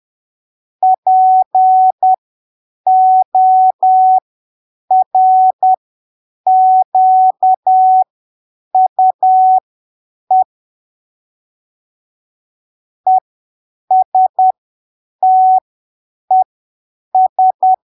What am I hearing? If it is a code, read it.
Morse code, heard as a keyed tone, about 10 wpm, spaced as for 7 wpm, 750 Hz. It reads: PORQUE ESTES
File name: arquivos / texto / texto 27 10ppm